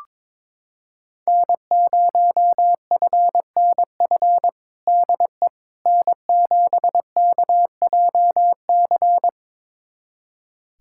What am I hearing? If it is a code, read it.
Morse code, heard as a keyed tone, about 21 wpm, 700 Hz. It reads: N0FNF DE N7KJC
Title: USA Callsign DE Callsign - 5 Characters - 22wpm